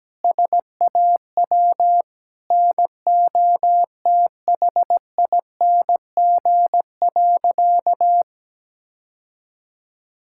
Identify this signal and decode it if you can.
Morse code, heard as a keyed tone, about 17 wpm, 700 Hz. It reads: SAW NOTHING.
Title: Sentences from Top 1000 Words - Review Part 6 - 17wpm